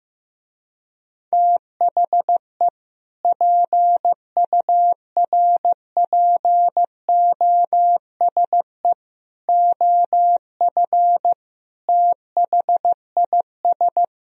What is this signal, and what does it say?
Morse code, heard as a keyed tone, about 15 wpm, 700 Hz. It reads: THE PURPOSE OF THIS
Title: Speed-Racing - Sets of 5 Words 15wpm